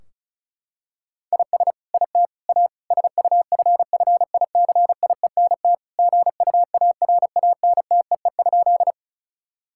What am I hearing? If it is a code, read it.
Morse code, heard as a keyed tone, about 36 wpm, 700 Hz. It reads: IS IT A SUFFICIENT GUARANTEE?
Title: Sentences from Top 1600 Words 35wpm